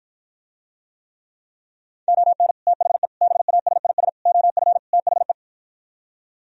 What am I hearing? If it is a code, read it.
Morse code, heard as a keyed tone, about 55 wpm, 700 Hz. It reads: ON THE BASIS OF THE